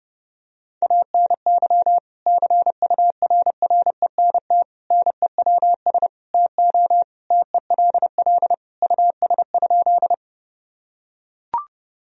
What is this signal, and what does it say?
Morse code, heard as a keyed tone, about 32 wpm, 700 Hz. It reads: ANY CURRENT NEWS TO TELL US?
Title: Sentences from Top 1000 Words - Review Part 4 - 30wpm